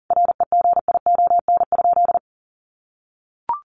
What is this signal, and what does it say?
Morse code, heard as a keyed tone, about 40 wpm, 700 Hz. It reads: REGION?
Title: Sentences from Top 1500 Words 40wpm